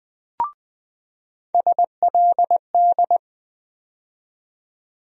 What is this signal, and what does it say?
Morse code, heard as a keyed tone, about 20 wpm, 700 Hz. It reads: SLD
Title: USA Call Sign Postfixes - 3 Letters 20wpm